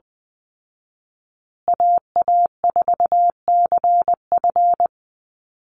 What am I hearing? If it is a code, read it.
Morse code, heard as a keyed tone, about 20 wpm, 700 Hz. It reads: AA4CF
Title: Call Signs Repeated 2x - 20wpm